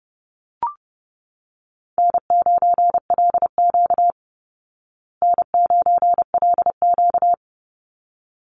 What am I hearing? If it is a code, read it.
Morse code, heard as a keyed tone, about 30 wpm, 700 Hz. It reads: N9LQ N9LQ
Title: Easy Call Signs Repeated 2x - 30wpm